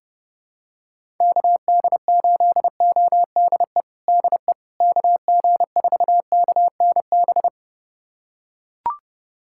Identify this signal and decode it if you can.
Morse code, heard as a keyed tone, about 30 wpm, 700 Hz. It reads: KD8ODE DE KG4KNB
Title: USA Callsign DE Callsign - 6 Characters - 30wpm